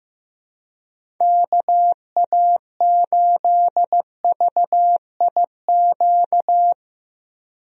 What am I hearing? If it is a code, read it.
Morse code, heard as a keyed tone, about 15 wpm, 700 Hz. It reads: KA8VIQ
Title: USA Call Signs - 6 Characters - 15wpm